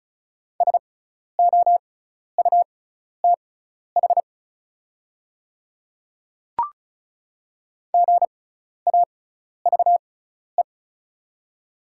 Morse code, heard as a keyed tone, about 34 wpm, spaced as for 12 wpm, 690 Hz. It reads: SOUTH GAVE
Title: 6x Character Spacing - Top 500 Words - 35wpm